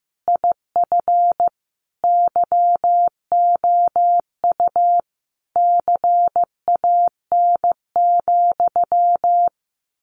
Morse code, heard as a keyed tone, about 15 wpm, 700 Hz. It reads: IF YOU CAN,
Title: A Sound Of Thunder 15wpm